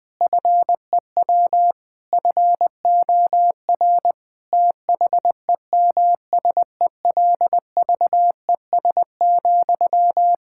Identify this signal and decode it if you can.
Morse code, heard as a keyed tone, about 20 wpm, 700 Hz. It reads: FEW FOR THEMSELVES,
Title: Sentences from Top 1000 Words - Review Part 6 - 20wpm